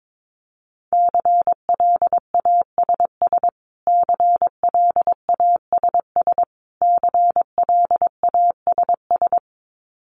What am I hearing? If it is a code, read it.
Morse code, heard as a keyed tone, about 22 wpm, 700 Hz. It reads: CLASS CLASS CLASS
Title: Top 500 Words Repeated 22wpm